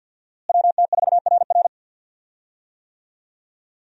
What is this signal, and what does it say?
Morse code, heard as a keyed tone, about 47 wpm, 700 Hz. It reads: WT4RR
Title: USA Call Signs - 5 Characters - Repeated 2x - 50wpm